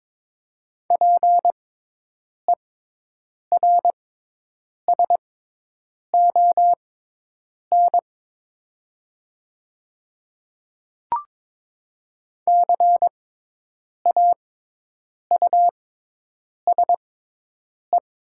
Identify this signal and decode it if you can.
Morse code, heard as a keyed tone, about 22 wpm, spaced as for 8 wpm, 700 Hz. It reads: PERSON CAUSE
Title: 6x Character Spacing - Top 500 Words - 22wpm